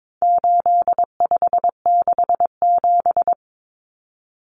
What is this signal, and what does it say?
Morse code, heard as a keyed tone, about 22 wpm, 700 Hz. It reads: 8567
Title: Numbers - 4-Digit 22wpm